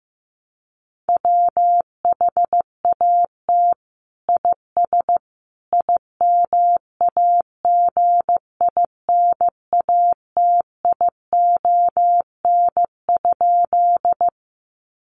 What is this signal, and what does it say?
Morse code, heard as a keyed tone, about 15 wpm, 700 Hz. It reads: WHAT IS IMAGINATION?
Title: Sentences from Top 1300 Words 15wpm